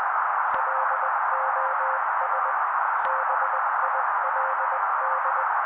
Morse code, heard as a keyed tone, about 20 wpm, 530 Hz. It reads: LOS BILD
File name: daily / 18wpm noise